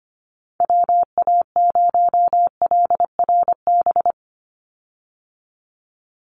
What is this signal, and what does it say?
Morse code, heard as a keyed tone, about 25 wpm, 700 Hz. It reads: WA0LRB
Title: USA Call Signs 25wpm